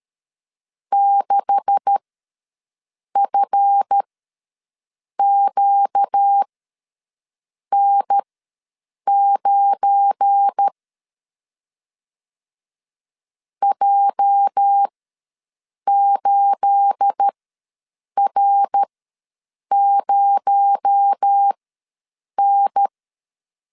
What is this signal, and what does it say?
Morse code, heard as a keyed tone, about 13 wpm, spaced as for 6 wpm, 800 Hz. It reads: 6FQN9 J8R0N
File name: LEZIONI / 10 SIGNS / 10D COMA ws 12wpm 3618